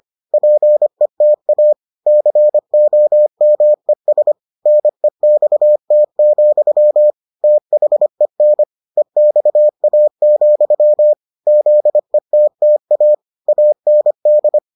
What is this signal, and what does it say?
Morse code, heard as a keyed tone, about 25 wpm, 600 Hz. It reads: PETA COMES NEXT, THEN EXA, ZETTA AND